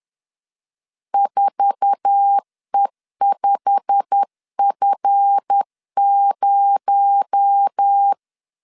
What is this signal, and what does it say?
Morse code, heard as a keyed tone, about 10 wpm, 800 Hz. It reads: 4E5F0
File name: LEZIONI / 8 YJ45 / 8 YJ45 10wpm 2549